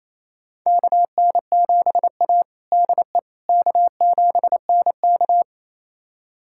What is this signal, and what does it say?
Morse code, heard as a keyed tone, about 27 wpm, 700 Hz. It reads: KN7A DE K7NK
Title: USA Callsign DE Callsign - 4 Characters - 28wpm